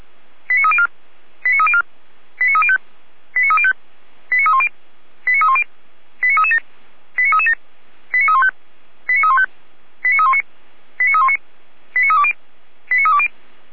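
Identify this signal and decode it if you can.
Five-tone selective call (ZVEI), heard as quick runs of five tones, tones of about 70 ms. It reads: 89304 89304 89305 89305 89310 89310 89337 89337 89315 89315 89319 89319 89320 89320
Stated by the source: ZVEI five-tone selective call 89265 received on SDR.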